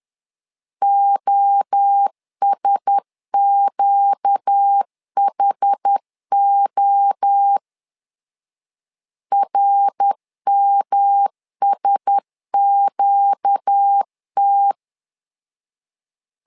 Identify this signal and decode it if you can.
Morse code, heard as a keyed tone, about 10 wpm, 800 Hz. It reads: OSQHO RMSQT
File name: LEZIONI / 4 Q1S 10wpm 2133